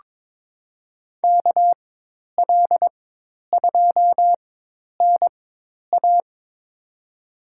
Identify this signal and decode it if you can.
Morse code, heard as a keyed tone, about 22 wpm, spaced as for 10 wpm, 700 Hz. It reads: KL2NA
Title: USA Call Signs - 5 Characters - 4x Character Spacing - 22wpm